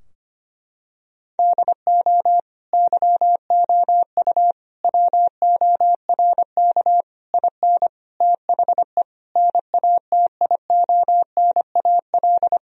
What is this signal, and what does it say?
Morse code, heard as a keyed tone, about 25 wpm, 700 Hz. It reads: DO YOU WORK IN THE NATIONAL
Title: Sentences from Top 1000 Words - Review Part 4 - 25wpm